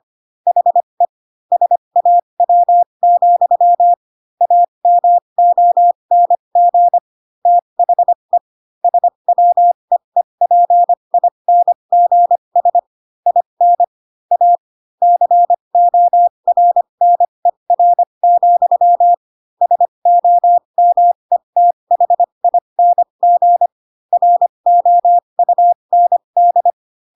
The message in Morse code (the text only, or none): HE SAW, AMONG THE SWEEPINGS IN A CORNER, SOMETHING ROUND